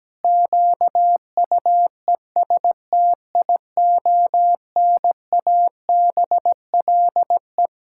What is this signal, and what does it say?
Morse code, heard as a keyed tone, about 17 wpm, 700 Hz. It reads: QUESTIONABLE
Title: Pangram Challenge v2 - 17wpm